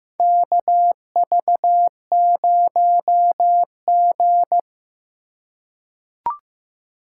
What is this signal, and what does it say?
Morse code, heard as a keyed tone, about 15 wpm, 700 Hz. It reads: KV0G
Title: USA Call Signs - 4 Characters - 4x Character Spacing - 15wpm